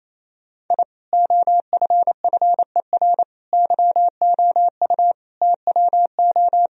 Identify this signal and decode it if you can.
Morse code, heard as a keyed tone, about 28 wpm, 700 Hz. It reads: I OFFER YOU TWO
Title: Sentences from Top 1000 Words - Review Part 4 - 28wpm